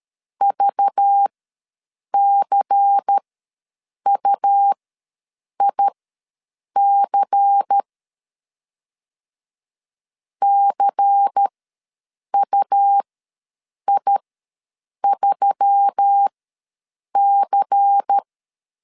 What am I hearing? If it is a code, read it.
Morse code, heard as a keyed tone, about 13 wpm, spaced as for 7 wpm, 800 Hz. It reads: VCUIC CUI3C
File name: LEZIONI / 5 LBVC / 5 LBVC ws 12wpm 3159